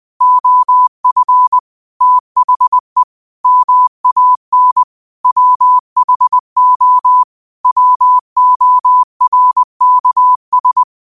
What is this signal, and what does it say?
Morse code, heard as a keyed tone, about 20 wpm, 1000 Hz. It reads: OF THE MAN WHO WORKS
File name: qotdfiles / 20wpm